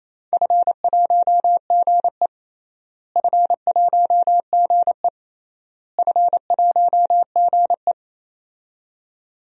Morse code, heard as a keyed tone, about 28 wpm, 700 Hz. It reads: F1GE F1GE F1GE
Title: DXCC-100 Call Signs - 4 Characters - Repeated 3x - 28wpm